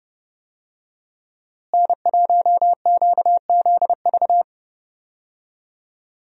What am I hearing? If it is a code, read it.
Morse code, heard as a keyed tone, about 30 wpm, 700 Hz. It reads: N1QZV
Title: USA Call Signs - 5 Characters - Repeated 2x - 30wpm